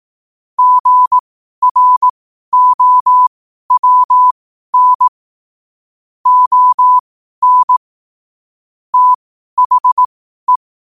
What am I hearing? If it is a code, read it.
Morse code, heard as a keyed tone, about 18 wpm, spaced as for 12 wpm, 1000 Hz. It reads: GROWN ON THE
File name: qotdfiles / 12wpm